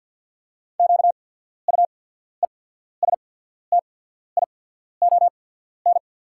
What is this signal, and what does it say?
Morse code, heard as a keyed tone, about 48 wpm, spaced as for 14 wpm, 690 Hz. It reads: QUESTION
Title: 8x Character Spacing - Top 500 Words - 50wpm